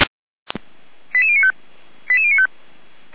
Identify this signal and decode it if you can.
Five-tone selective call (ZVEI), heard as quick runs of five tones, tones of about 80 ms. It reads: 88095 88095